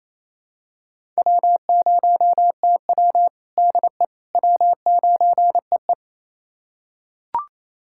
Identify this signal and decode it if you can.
Morse code, heard as a keyed tone, about 28 wpm, 700 Hz. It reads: W0TW DE W9EE
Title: USA Callsign DE Callsign - 4 Characters - 28wpm